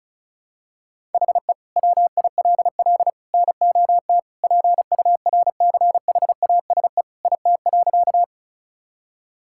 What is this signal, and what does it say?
Morse code, heard as a keyed tone, about 36 wpm, 700 Hz. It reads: HE WILL NOT PURCHASE IT.
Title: Sentences from Top 1000 Words - Review Part 5 - 35wpm